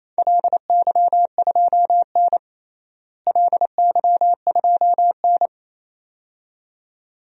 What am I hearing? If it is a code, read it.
Morse code, heard as a keyed tone, about 28 wpm, 700 Hz. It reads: LY2N LY2N
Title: DXCC-100 Call Signs - 4 Characters - Repeated 2x - 28wpm